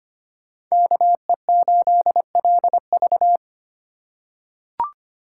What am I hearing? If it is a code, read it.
Morse code, heard as a keyed tone, about 25 wpm, 700 Hz. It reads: KE8LV
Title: USA Call Signs - 5 Characters - 2x Character Spacing - 25wpm